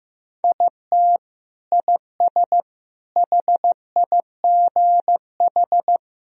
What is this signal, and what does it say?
Morse code, heard as a keyed tone, about 15 wpm, 700 Hz. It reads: IT IS HIGH
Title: Sentences from Top 1000 Words - Review Part 1 - 15wpm